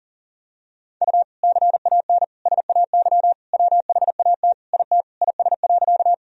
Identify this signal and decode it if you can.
Morse code, heard as a keyed tone, about 37 wpm, 700 Hz. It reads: U CAN SAY WHAT IT IS.
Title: Sentences from Top 200 Words and Abbreviations 40wpm